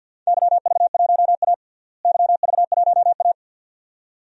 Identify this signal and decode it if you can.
Morse code, heard as a keyed tone, about 45 wpm, 700 Hz. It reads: YV1A YV1A
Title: DXCC-100 Call Signs - 4 Characters - Repeated 2x - 50wpm